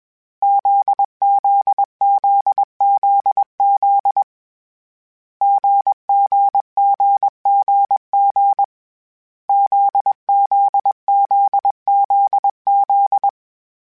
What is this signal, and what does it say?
Morse code, heard as a keyed tone, about 21 wpm, 800 Hz. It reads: ZZZZZ GGGGG ZZZZZ
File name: Short Runs / Short Runs-mp3s / GZ GZ7 @20